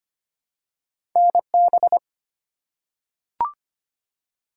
Morse code, heard as a keyed tone, about 25 wpm, 700 Hz. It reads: NB
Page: morse-code-ninja-practice-25wpm